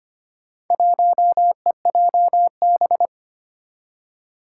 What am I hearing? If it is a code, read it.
Morse code, heard as a keyed tone, about 25 wpm, 700 Hz. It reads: 1EJB
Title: Four Challenging Letter-Number Combinations 25wpm